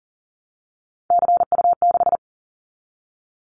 Morse code, heard as a keyed tone, about 38 wpm, 700 Hz. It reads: CU6
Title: Callsign Prefixes for Top 50 DXCC - 40wpm